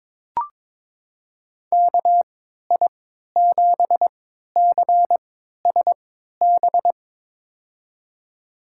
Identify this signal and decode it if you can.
Morse code, heard as a keyed tone, about 22 wpm, spaced as for 13 wpm, 700 Hz. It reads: KI7CSB
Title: USA Call Signs - 3x Character Spacing - 22wpm